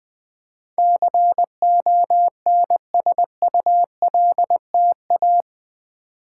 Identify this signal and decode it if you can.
Morse code, heard as a keyed tone, about 20 wpm, 700 Hz. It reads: CONSULTA
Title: Build A Word - 4 Letters At A Time - 10-Letter Words - 20wpm